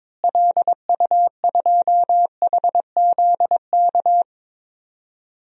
Morse code, heard as a keyed tone, about 22 wpm, 700 Hz. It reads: LU2HZK
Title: DXCC-100 Call Signs - 6 Characters - Repeated 3x - 22wpm